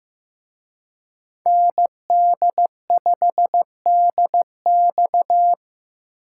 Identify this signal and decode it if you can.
Morse code, heard as a keyed tone, about 18 wpm, 700 Hz. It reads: ND5DX
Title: Speed Racing - USA 2x2 Call Signs 15wpm